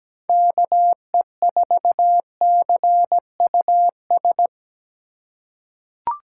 Morse code, heard as a keyed tone, about 17 wpm, 700 Hz. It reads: KE4CUS
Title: USA Call Signs - 2x Character Spacing - 17wpm